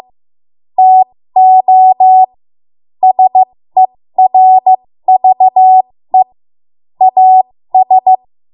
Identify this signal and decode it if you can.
Morse code, heard as a keyed tone, about 15 wpm, spaced as for 13 wpm, 750 Hz. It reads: TO SERVE AS
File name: Morse / 13WPM